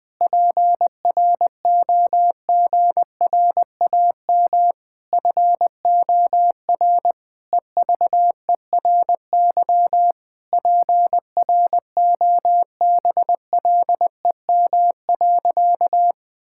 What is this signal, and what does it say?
Morse code, heard as a keyed tone, about 20 wpm, 700 Hz. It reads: PROGRAM FOR EVERY PROBLEM.
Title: Sentences from Top 1000 Words - Review Part 5 - 20wpm